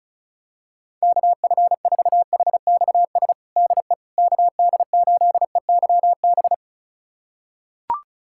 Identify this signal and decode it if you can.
Morse code, heard as a keyed tone, about 36 wpm, 700 Hz. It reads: KF4HXS DE KD8EYB
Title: USA Callsign DE Callsign - 6 Characters - 35wpm